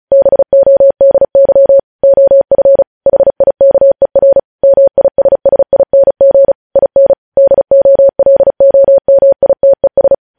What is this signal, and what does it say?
Morse code, heard as a keyed tone, about 36 wpm, 550 Hz. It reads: BODY OF HIKER MISSING IN DOLOMITES
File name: podcast / News headlines-1x-35x35